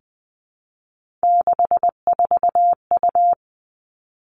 Morse code, heard as a keyed tone, about 20 wpm, 700 Hz. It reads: 64U
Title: Three Challenging Letter-Number Combinations 20wpm